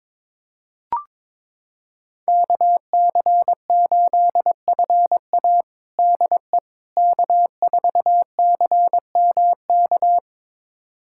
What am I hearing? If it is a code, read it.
Morse code, heard as a keyed tone, about 22 wpm, 700 Hz. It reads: KC8FA DE K4CMK
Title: USA Callsign DE Callsign - 5 Characters - 22wpm